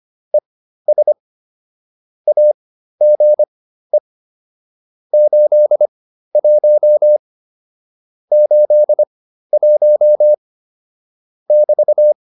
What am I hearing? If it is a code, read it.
Morse code, heard as a keyed tone, about 25 wpm, spaced as for 13 wpm, 600 Hz. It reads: ES AGE 81 81 <BT>